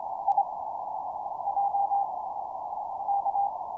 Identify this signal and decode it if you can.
Morse code, heard as a keyed tone, about 25 wpm, 800 Hz. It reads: I J K
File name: signalNoise / MorseABC BPF100Hz